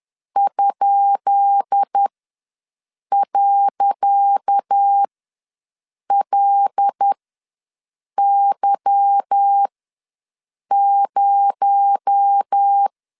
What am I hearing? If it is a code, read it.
Morse code, heard as a keyed tone, about 10 wpm, spaced as for 6 wpm, 800 Hz. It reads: ?.LY0
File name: LEZIONI / 10 SIGNS / 10C DOT ws 10wpm 4213